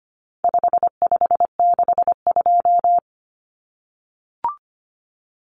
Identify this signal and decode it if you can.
Morse code, heard as a keyed tone, about 25 wpm, 700 Hz. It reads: 5562
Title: Common Cut Numbers - 4-Digit 25wpm